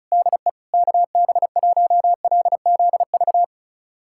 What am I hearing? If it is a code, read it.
Morse code, heard as a keyed tone, about 36 wpm, 700 Hz. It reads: DE KB1LZV
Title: USA Callsign DE Callsign - 6 Characters - 35wpm